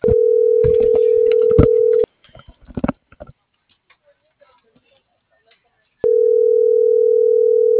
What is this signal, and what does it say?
Ringback tone (North America), 440 + 480 Hz.